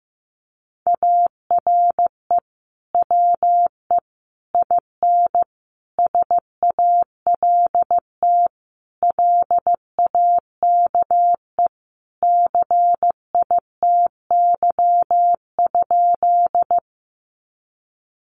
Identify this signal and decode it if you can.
Morse code, heard as a keyed tone, about 15 wpm, 700 Hz. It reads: ARE WE IN SALT LAKE CITY?
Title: Sentences from Top 1100 Words 15wpm